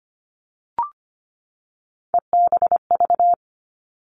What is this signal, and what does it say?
Morse code, heard as a keyed tone, about 25 wpm, 700 Hz. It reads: EBV
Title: Three Challenging Letter-Number Combinations 25wpm